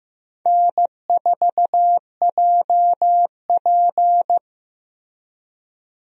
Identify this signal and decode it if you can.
Morse code, heard as a keyed tone, about 15 wpm, 700 Hz. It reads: N4JP
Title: USA Call Signs - 4 Characters - 15wpm